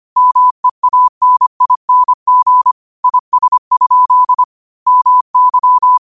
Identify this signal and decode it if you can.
Morse code, heard as a keyed tone, about 25 wpm, 1000 Hz. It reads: MEANING IS? MY
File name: qotdfiles / 25wpm